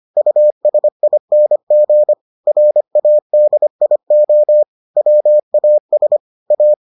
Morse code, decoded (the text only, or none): USING RADIO WAS A